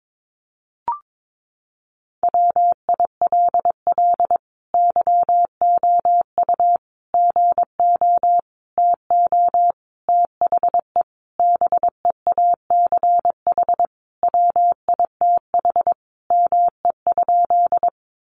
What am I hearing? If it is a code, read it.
Morse code, heard as a keyed tone, about 22 wpm, 700 Hz. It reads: WILL YOU GO TO THE BEACH WITH ME?
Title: Sentences from Top 1500 Words 22wpm